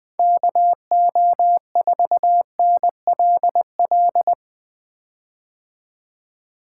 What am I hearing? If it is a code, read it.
Morse code, heard as a keyed tone, about 20 wpm, 700 Hz. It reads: KO4NLL
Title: 2023 POTA Activator Call Signs - 20wpm